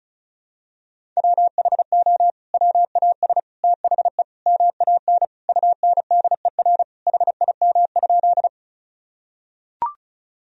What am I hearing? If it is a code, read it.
Morse code, heard as a keyed tone, about 36 wpm, 700 Hz. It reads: WHO WAS THE MAN UNDER HIM?
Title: Sentences from Top 1000 Words - Review Part 1 - 35wpm